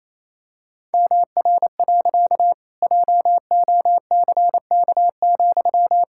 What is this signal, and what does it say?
Morse code, heard as a keyed tone, about 28 wpm, 700 Hz. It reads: MR. JOCK,
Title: Pangram Challenge v2 - 28wpm